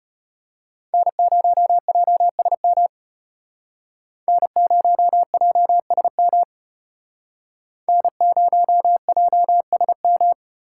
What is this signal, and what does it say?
Morse code, heard as a keyed tone, about 32 wpm, 700 Hz. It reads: N0JSM N0JSM N0JSM